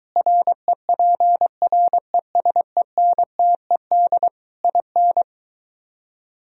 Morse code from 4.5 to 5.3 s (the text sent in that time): IN